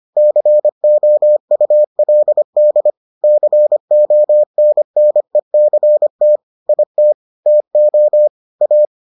Morse code, heard as a keyed tone, about 25 wpm, 600 Hz. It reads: COULD CONNECT IT TO A